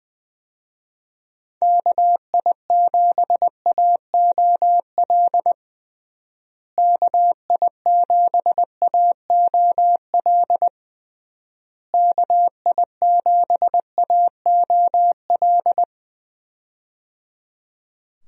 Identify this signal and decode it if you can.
Morse code, heard as a keyed tone, about 20 wpm, 700 Hz. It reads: KI7AOL KI7AOL KI7AOL
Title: USA Call Signs - 6 Characters - Repeated 3x - 20wpm